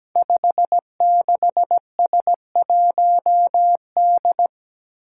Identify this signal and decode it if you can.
Morse code, heard as a keyed tone, about 17 wpm, 700 Hz. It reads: 56S1D
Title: Five Challenging Letter-Number Combinations 17wpm